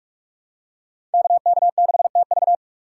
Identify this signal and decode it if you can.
Morse code, heard as a keyed tone, about 47 wpm, 700 Hz. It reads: KK6TV
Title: K1USN SST QSO Practice - 45wpm